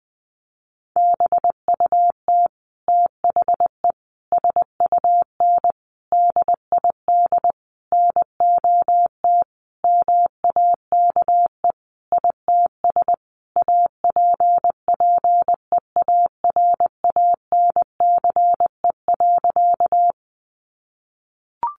Morse code, heard as a keyed tone, about 20 wpm, 700 Hz. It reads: BUT THE SUN DID NOT MAKE ITS APPEARANCE.
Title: Sentences from Top 1100 Words 20wpm